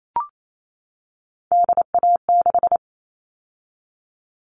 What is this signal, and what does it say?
Morse code, heard as a keyed tone, about 28 wpm, 700 Hz. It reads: DA6
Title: All Cut Numbers - 3-Digit 28wpm